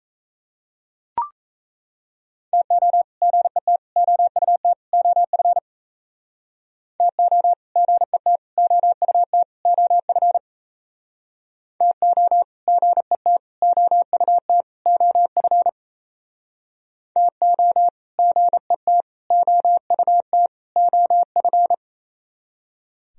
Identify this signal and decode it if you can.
Morse code, heard as a keyed tone, about 37 wpm, 700 Hz. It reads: TO GET OUT OF TO GET OUT OF TO GET OUT OF TO GET OUT OF